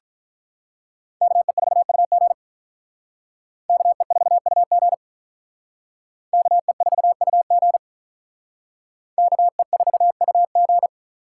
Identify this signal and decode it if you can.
Morse code, heard as a keyed tone, about 44 wpm, 700 Hz. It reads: KE4UG KE4UG KE4UG KE4UG